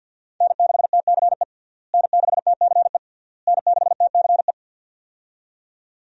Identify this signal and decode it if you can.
Morse code, heard as a keyed tone, about 51 wpm, 700 Hz. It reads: N6TCE N6TCE N6TCE
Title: USA Call Signs - 5 Characters - Repeated 3x - 50wpm